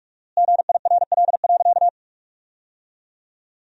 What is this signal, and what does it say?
Morse code, heard as a keyed tone, about 46 wpm, 700 Hz. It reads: GIRL.
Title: 4x Word-Spacing - Sentences from Top 400 Words - 45wpm